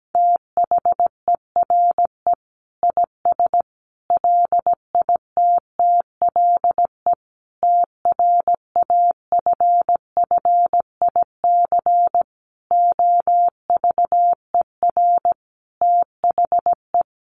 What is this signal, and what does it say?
Morse code, heard as a keyed tone, about 17 wpm, 700 Hz. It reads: THERE IS LITTLE TRAFFIC OVER THE
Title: Sentences from Top 1300 Words 17wpm